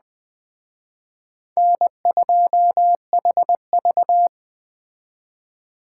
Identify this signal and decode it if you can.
Morse code, heard as a keyed tone, about 20 wpm, 700 Hz. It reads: N2HV
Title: Speed Racing - USA 1x2 Call Signs 17wpm